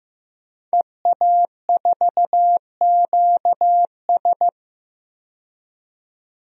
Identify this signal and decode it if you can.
Morse code, heard as a keyed tone, about 15 wpm, 700 Hz. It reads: EA4QS
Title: DXCC-100 Call Signs - 5 Characters - 15wpm